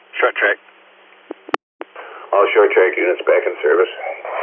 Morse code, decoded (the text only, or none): none